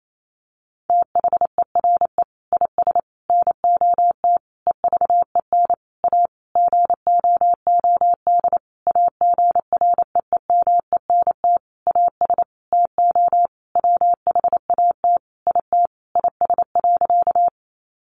Morse code, heard as a keyed tone, about 28 wpm, 700 Hz. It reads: THERE IS NOT EVEN A GOOD AGREEMENT AS TO WHAT IT IS.
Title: Sentences from Top 1500 Words 28wpm